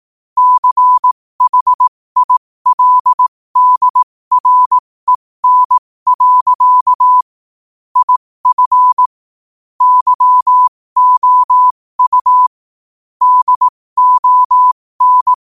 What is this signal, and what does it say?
Morse code, heard as a keyed tone, about 18 wpm, spaced as for 15 wpm, 1000 Hz. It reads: CHILDREN. IF YOU DON
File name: qotdfiles / 15wpm